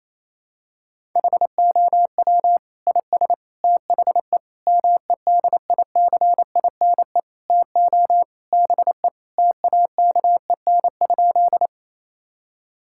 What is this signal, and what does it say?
Morse code, heard as a keyed tone, about 28 wpm, 700 Hz. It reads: HOW IS THE MEDICINE TO BE TAKEN?
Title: Sentences from Top 1000 Words - Review Part 6 - 28wpm